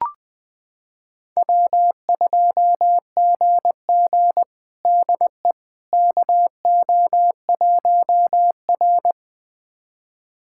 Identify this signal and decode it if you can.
Morse code, heard as a keyed tone, about 20 wpm, 700 Hz. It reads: W2GG DE KO1R
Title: USA Callsign DE Callsign - 4 Characters - 20wpm